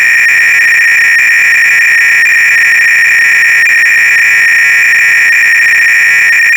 300-baud modem data.